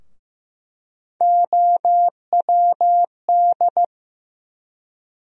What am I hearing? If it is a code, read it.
Morse code, heard as a keyed tone, about 15 wpm, 700 Hz. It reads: OWD
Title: USA Call Sign Postfixes - 3 Letters 15wpm